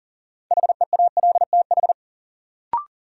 Morse code, heard as a keyed tone, about 40 wpm, 700 Hz. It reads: HEALTH
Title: Sets of 5 Words - New Word Drill - 40wpm